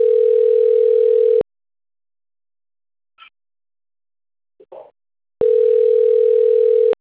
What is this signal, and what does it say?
Ringback tone (North America), 440 + 480 Hz.